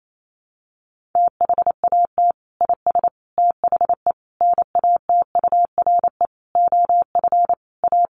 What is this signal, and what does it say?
Morse code, heard as a keyed tone, about 28 wpm, 700 Hz. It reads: THAT IS THE NATURE OF A
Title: Sentences from Top 1400 Words 28wpm